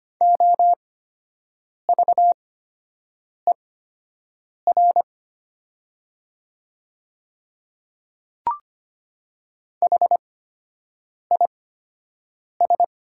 Morse code, heard as a keyed tone, about 25 wpm, spaced as for 7 wpm, 700 Hz. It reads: OVER HIS
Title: 8x Character Spacing - Top 300 Words - 25wpm